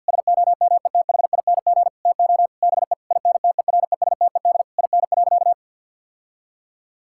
Morse code, heard as a keyed tone, about 50 wpm, 700 Hz. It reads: SOMETHING TO BE INTERESTED IN.
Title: Sentences from Top 1000 Words - Review Part 6 - 50wpm